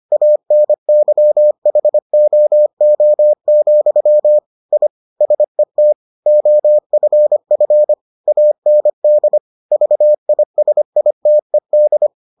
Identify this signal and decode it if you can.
Morse code, heard as a keyed tone, about 25 wpm, 600 Hz. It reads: ANYHOO, I SET OFF AND VISITED